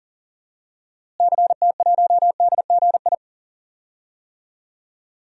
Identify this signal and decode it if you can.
Morse code, heard as a keyed tone, about 40 wpm, 700 Hz. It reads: CT1DGI
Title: DXCC-100 Call Signs - 6 Characters - Repeated 2x - 40wpm